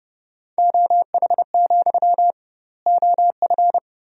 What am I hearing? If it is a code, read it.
Morse code, heard as a keyed tone, about 30 wpm, 700 Hz. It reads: OH, OF
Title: 2x Word-Spacing - Sentences from Top 500 Words - 30wpm